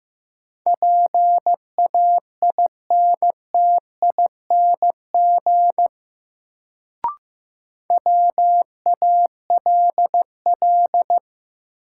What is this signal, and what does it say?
Morse code, heard as a keyed tone, about 15 wpm, 700 Hz. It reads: PAINTING WALL
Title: Sentences from Top 1000 Words - Review Part 6 - New Word Drill - 15wpm